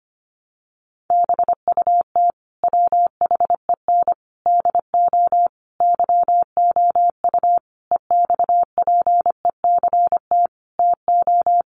Morse code, heard as a keyed tone, about 25 wpm, 690 Hz. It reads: BUT WHEN DO YOU EXPECT TO
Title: Sentences from Top 1100 Words 25wpm